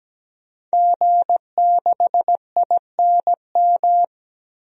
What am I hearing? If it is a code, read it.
Morse code, heard as a keyed tone, about 17 wpm, 700 Hz. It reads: G6INM
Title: DXCC-100 Call Signs - 5 Characters - Repeated 2x - 17wpm